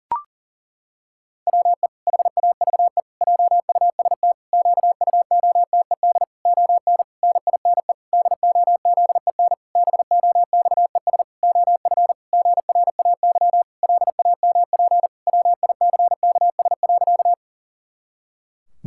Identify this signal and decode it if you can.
Morse code, heard as a keyed tone, about 40 wpm, 700 Hz. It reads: WE HAVE JUST QUOTED ON NINE DOZEN BOXES OF GRAY LAMP WICKS.
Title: Pangram Challenge v2 - 40wpm